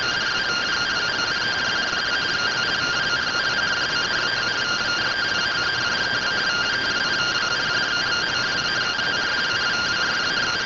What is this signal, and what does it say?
Two-tone teleprinter signal, about 170 Hz shift.